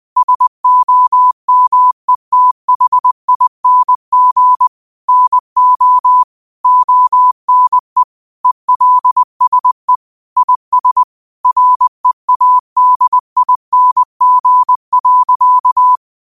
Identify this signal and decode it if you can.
Morse code, heard as a keyed tone, about 20 wpm, 1000 Hz. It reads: SOMETHING NO ONE ELSE IS READING.
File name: qotdfiles / 20wpm